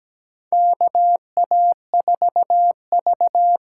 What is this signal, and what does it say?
Morse code, heard as a keyed tone, about 17 wpm, 700 Hz. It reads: KA4V
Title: USA Call Signs - 8x Character Spacing - 17wpm